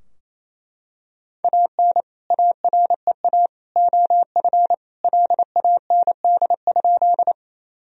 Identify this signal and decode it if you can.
Morse code, heard as a keyed tone, about 28 wpm, 700 Hz. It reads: AN AREA OF LAND?
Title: Sentences from Top 1000 Words - Review Part 2 - 28wpm